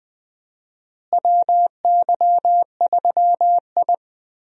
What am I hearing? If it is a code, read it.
Morse code, heard as a keyed tone, about 21 wpm, 700 Hz. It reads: WY3I
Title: USA Call Signs - 4 Characters - Repeated 3x - 20wpm